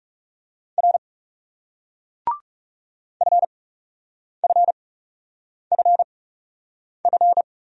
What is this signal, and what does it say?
Morse code, heard as a keyed tone, about 40 wpm, spaced as for 8 wpm, 700 Hz. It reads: R FFFF